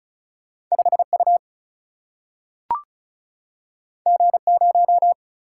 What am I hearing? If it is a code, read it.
Morse code, heard as a keyed tone, about 36 wpm, 700 Hz. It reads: 5U G0
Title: Single-Letter Single-Number 35wpm